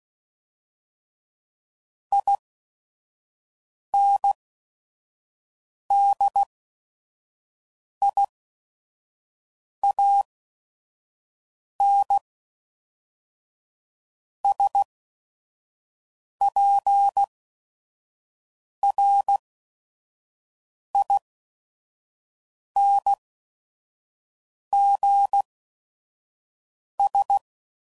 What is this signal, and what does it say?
Morse code, heard as a keyed tone, about 16 wpm, spaced as for 5 wpm, 800 Hz. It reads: INDIANSPRINGS
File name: CW Code Groups @5WPM / fromSM's / Cities